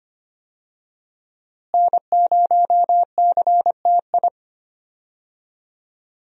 Morse code, heard as a keyed tone, about 25 wpm, 700 Hz. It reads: N0CTI
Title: USA Call Signs - 5 Characters - Repeated 3x - 25wpm